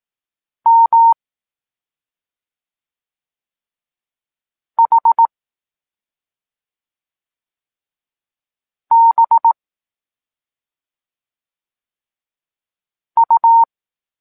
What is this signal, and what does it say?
Morse code, heard as a keyed tone, about 18 wpm, 920 Hz. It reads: M H B U
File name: Tekst-23 4